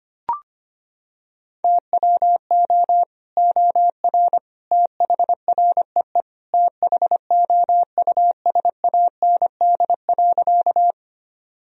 Morse code, heard as a keyed tone, about 25 wpm, 700 Hz. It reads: TWO OR THREE THOUSAND.
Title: Sentences from Top 1000 Words - Review Part 3 - 25wpm